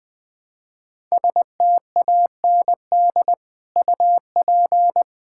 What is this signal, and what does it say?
Morse code, heard as a keyed tone, about 21 wpm, 700 Hz. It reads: STAND UP
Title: Sentences from Top 1000 Words - Review Part 6 - 20wpm